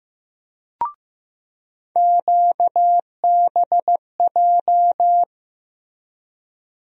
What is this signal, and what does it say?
Morse code, heard as a keyed tone, about 15 wpm, 700 Hz. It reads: QBJ
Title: USA Call Sign Postfixes - 3 Letters 15wpm